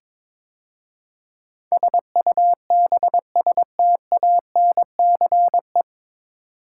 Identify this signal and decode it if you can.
Morse code, heard as a keyed tone, about 22 wpm, 700 Hz. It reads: SUBSTANCE
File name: 9 Letter Words - First 3 Letters Given - 22wpm